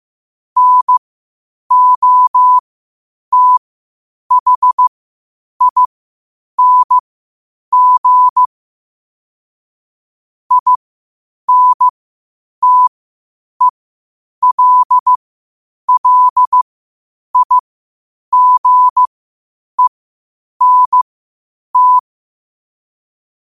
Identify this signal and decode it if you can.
Morse code, heard as a keyed tone, about 15 wpm, spaced as for 8 wpm, 1000 Hz. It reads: NOTHING INTELLIGENT
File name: qotdfiles / 8wpm